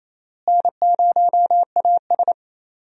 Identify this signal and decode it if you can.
Morse code, heard as a keyed tone, about 28 wpm, 700 Hz. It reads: N0AS
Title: USA Call Signs - 4 Characters - Repeated 2x - 28wpm